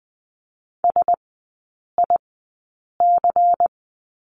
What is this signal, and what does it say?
Morse code, heard as a keyed tone, about 20 wpm, spaced as for 8 wpm, 700 Hz. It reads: SIC
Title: 093 Lesson 16e - Letter c - Three Characters 20wpm